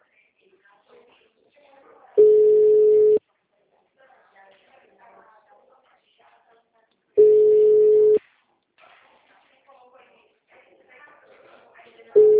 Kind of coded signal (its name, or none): telephone call progress tone